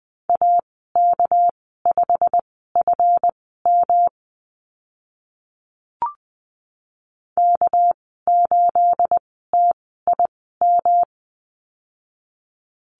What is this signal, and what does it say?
Morse code, heard as a keyed tone, about 20 wpm, spaced as for 14 wpm, 700 Hz. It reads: AK5FM K8TIM
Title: 231 Lesson 37g - Letter x - Call Signs 20wpm